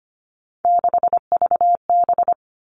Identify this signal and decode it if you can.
Morse code, heard as a keyed tone, about 25 wpm, 700 Hz. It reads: 6VB
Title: All Cut Numbers - 3-Digit 25wpm